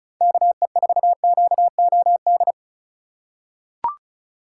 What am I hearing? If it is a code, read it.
Morse code, heard as a keyed tone, about 36 wpm, 700 Hz. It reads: KE4QOD
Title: USA Call Signs - 6 Characters - 4x Character Spacing - 35wpm